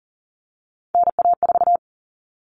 Morse code, heard as a keyed tone, about 40 wpm, 700 Hz. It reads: NA4
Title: All Cut Numbers - 3-Digit 40wpm